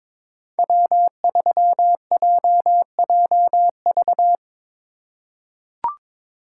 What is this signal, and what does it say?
Morse code, heard as a keyed tone, about 22 wpm, 700 Hz. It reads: W3JJV
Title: USA Call Signs - 5 Characters - 4x Character Spacing - 22wpm